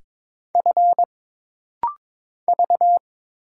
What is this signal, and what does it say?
Morse code, heard as a keyed tone, about 22 wpm, 700 Hz. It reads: F V
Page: This rapid-fire timing is geared to push your ICR (Instant Character Recognition) skills.